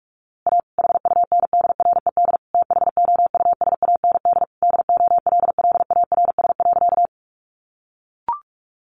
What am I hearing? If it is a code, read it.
Morse code, heard as a keyed tone, about 46 wpm, 700 Hz. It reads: A HUNDRED THOUSAND DOLLARS.
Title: A Sound Of Thunder 45wpm